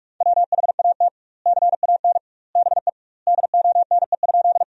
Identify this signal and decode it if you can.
Morse code, heard as a keyed tone, about 43 wpm, 700 Hz. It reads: WHAT CAN BE DONE?
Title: 2x Word-Spacing - Sentences from Top 500 Words - 45wpm